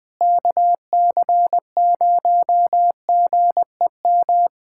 Morse code, heard as a keyed tone, about 20 wpm, 700 Hz. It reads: KC0GEM
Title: USA Callsign DE Callsign 20wpm